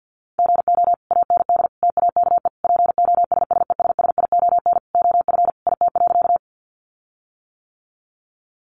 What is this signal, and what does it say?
Morse code, heard as a keyed tone, about 50 wpm, 700 Hz. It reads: GO AND TAKE POSSESSION OF IT.
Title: Sentences from Top 1400 Words 50wpm